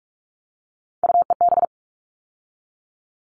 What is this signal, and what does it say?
Morse code, heard as a keyed tone, about 44 wpm, 700 Hz. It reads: UEB
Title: All Cut Numbers - 3-Digit 45wpm